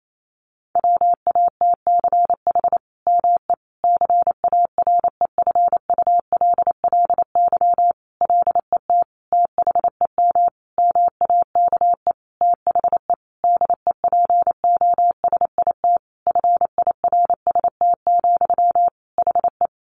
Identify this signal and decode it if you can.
Morse code, heard as a keyed tone, about 28 wpm, 700 Hz. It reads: WATCH ME CAREFULLY LET THEM MAKE THE DEPOSIT FIRST, HE
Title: Sentences from Top 1300 Words 28wpm